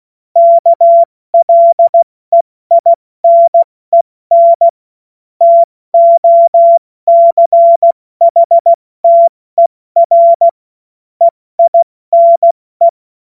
Morse code, heard as a keyed tone, about 16 wpm, spaced as for 14 wpm, 680 Hz. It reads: KLEINEN TOCHTER EINE
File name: Morsekurs / Witze / Witz 31. Januar 2023 70 F80